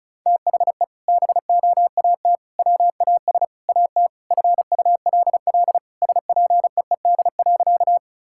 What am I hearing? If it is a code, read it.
Morse code, heard as a keyed tone, about 34 wpm, 700 Hz. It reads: THE BOAT WAS AT FULL SPEED.
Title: Sentences from Top 1000 Words - Review Part 5 - 35wpm